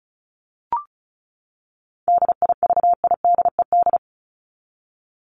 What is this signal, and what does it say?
Morse code, heard as a keyed tone, about 34 wpm, 700 Hz. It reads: DIVIDED
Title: ED Suffix - Top 100 Words with Base Words 35wpm